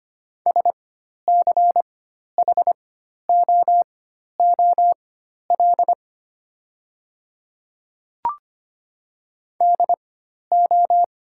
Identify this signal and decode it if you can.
Morse code, heard as a keyed tone, about 25 wpm, spaced as for 12 wpm, 700 Hz. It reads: SCHOOL DO
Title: 4x Character Spacing - Top 300 Words - 25wpm